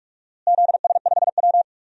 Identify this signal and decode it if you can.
Morse code, heard as a keyed tone, about 41 wpm, 700 Hz. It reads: ZS5W
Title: DXCC-100 Call Signs - 4 Characters - Repeated 2x - 45wpm